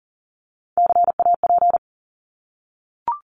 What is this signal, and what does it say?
Morse code, heard as a keyed tone, about 39 wpm, 700 Hz. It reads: CAP
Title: Top 1501-1750 Words 40wpm